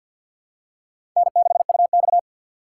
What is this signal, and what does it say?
Morse code, heard as a keyed tone, about 53 wpm, 700 Hz. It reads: N6UX
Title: USA Call Signs - 4 Characters - 2x Character Spacing - 50wpm